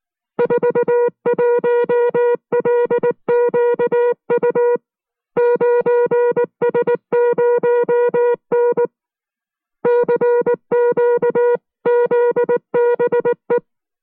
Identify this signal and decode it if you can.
Morse code, heard as a keyed tone, about 19 wpm, 480 Hz. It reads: 41LQU 9S0N CQZBE